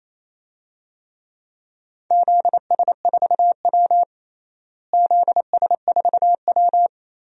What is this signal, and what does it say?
Morse code, heard as a keyed tone, about 28 wpm, 700 Hz. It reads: ZS4W ZS4W
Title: DXCC-100 Call Signs - 4 Characters - Repeated 3x - 28wpm